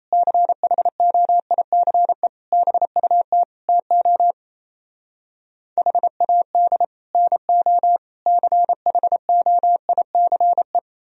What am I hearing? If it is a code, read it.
Morse code, heard as a keyed tone, about 37 wpm, 700 Hz. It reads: CHOICE BUT TO HAD NO CHOICE